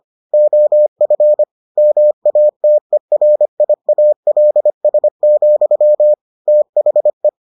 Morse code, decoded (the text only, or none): OF MATERIALS, THE